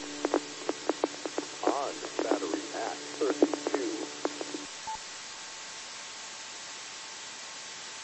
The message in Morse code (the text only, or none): none